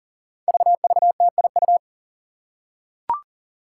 Morse code, heard as a keyed tone, about 43 wpm, 700 Hz. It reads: VVTIU
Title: Five Challenging Letter-Number Combinations 40wpm